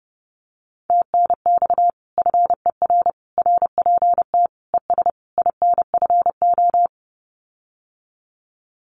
Morse code, heard as a keyed tone, about 30 wpm, 700 Hz. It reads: TNX FER RPT ES INFO
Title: Common QSO Phrases Encore 30wpm